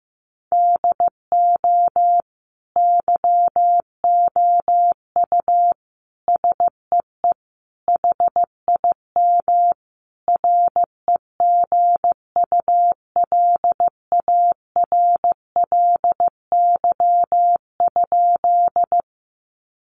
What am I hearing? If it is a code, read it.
Morse code, heard as a keyed tone, about 15 wpm, 700 Hz. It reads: DO YOU SEE HIM REGULARLY?
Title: Sentences from Top 1400 Words 15wpm